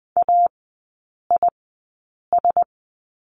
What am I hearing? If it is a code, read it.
Morse code, heard as a keyed tone, about 20 wpm, spaced as for 8 wpm, 700 Hz. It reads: AIS
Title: 053 Lesson 10e - Letter r - Three Characters 20wpm